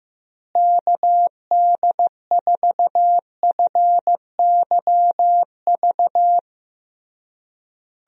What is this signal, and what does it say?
Morse code, heard as a keyed tone, about 15 wpm, 700 Hz. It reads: KD4FYV
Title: USA Call Signs - 6 Characters - Repeated 3x - 15wpm